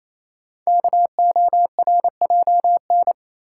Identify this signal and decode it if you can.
Morse code, heard as a keyed tone, about 27 wpm, 700 Hz. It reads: KORJN
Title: Code Groups - 5 Letters - 28wpm